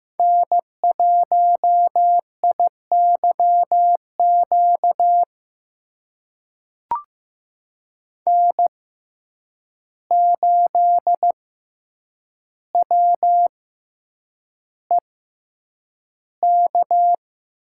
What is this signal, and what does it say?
Morse code, heard as a keyed tone, about 16 wpm, spaced as for 8 wpm, 700 Hz. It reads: N1IYQ N8WEK
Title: USA Call Signs - 5 Characters - 6x Character Spacing - 15wpm